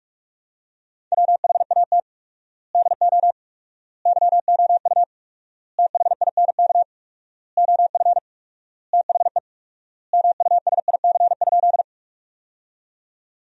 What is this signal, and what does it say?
Morse code, heard as a keyed tone, about 42 wpm, 700 Hz. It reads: WHAT DO YOU THINK OF THE MUSIC?
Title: 4x Word-Spacing - Sentences from Top 300 Words - 45wpm